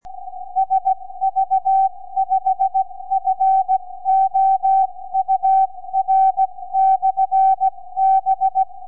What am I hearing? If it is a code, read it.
Morse code, heard as a keyed tone, about 16 wpm, spaced as for 14 wpm, 770 Hz. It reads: SV5FOUR/B
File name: sv5four 4m